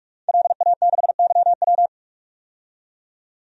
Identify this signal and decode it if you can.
Morse code, heard as a keyed tone, about 41 wpm, 700 Hz. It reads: LA6YW
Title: DXCC-100 Call Signs - 5 Characters - 45wpm